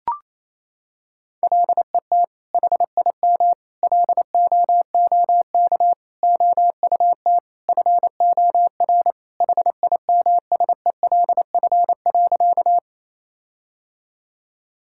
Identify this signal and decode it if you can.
Morse code, heard as a keyed tone, about 28 wpm, 700 Hz. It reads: LET HIM LOOK OUT FOR HIMSELF.
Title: Sentences from Top 1600 Words 28wpm